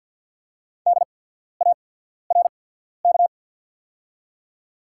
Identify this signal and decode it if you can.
Morse code, heard as a keyed tone, about 48 wpm, spaced as for 14 wpm, 700 Hz. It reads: DARK
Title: 8x Character Spacing - Top 500 Words - 50wpm